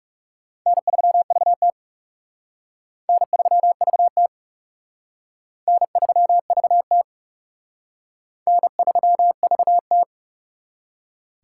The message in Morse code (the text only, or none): N3VT N3VT N3VT N3VT